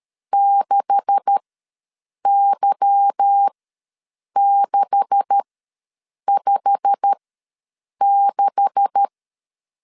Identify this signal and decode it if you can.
Morse code, heard as a keyed tone, about 13 wpm, spaced as for 7 wpm, 800 Hz. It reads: 6Y656
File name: LEZIONI / 9 286 ws 12wpm 3351